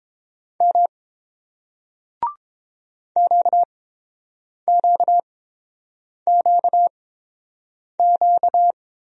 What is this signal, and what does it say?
Morse code, heard as a keyed tone, about 31 wpm, spaced as for 7 wpm, 700 Hz. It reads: M QQQQ